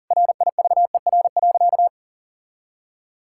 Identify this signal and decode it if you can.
Morse code, heard as a keyed tone, about 40 wpm, 700 Hz. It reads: RIVER.
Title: Sentences from Top 1000 Words - Review Part 6 - 40wpm